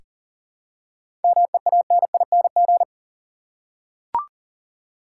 Morse code, heard as a keyed tone, about 40 wpm, 700 Hz. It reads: MEANING
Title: Seven Letter Words - 40wpm